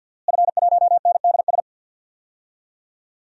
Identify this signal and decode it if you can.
Morse code, heard as a keyed tone, about 49 wpm, 700 Hz. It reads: F1NDS
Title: DXCC-100 Call Signs - 5 Characters - Repeated 3x - 50wpm